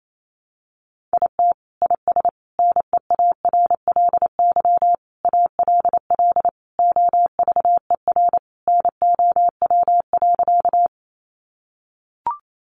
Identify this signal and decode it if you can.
Morse code, heard as a keyed tone, about 28 wpm, 700 Hz. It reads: IT IS NEARLY ALL OVER NOW.
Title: Sentences from Top 1300 Words 28wpm